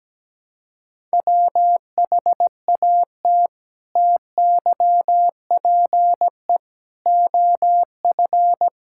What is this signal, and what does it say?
Morse code, heard as a keyed tone, about 17 wpm, 700 Hz. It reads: WHAT TYPE OF
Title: Sentences from Top 1000 Words - Review Part 6 - 17wpm